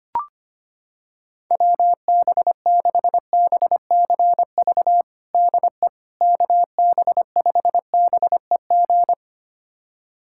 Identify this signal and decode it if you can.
Morse code, heard as a keyed tone, about 25 wpm, 700 Hz. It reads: WB6BCV DE KB5BEG
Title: USA Callsign DE Callsign - 6 Characters - 25wpm